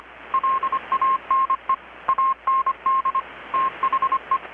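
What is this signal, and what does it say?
Morse code, heard as a keyed tone, about 25 wpm, 1070 Hz, with radio noise. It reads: LANE AND THE
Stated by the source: Morse from KPH's Night of Nights XX maritime radiotelegraphy event, received in Chiapas, Mexico about 3,800 km away on a Kenwood R-600 with a grounded 10 m long-wire antenna.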